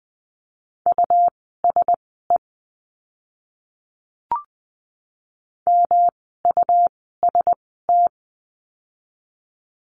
Morse code, heard as a keyed tone, about 20 wpm, spaced as for 14 wpm, 700 Hz. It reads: USE MUST
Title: 237 Lesson 38f - Letter q - Words 20wpm